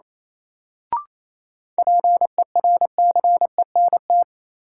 Morse code, heard as a keyed tone, about 27 wpm, 700 Hz. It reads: PERCENT
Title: Sets of 2 Words - New Word Drill - 28wpm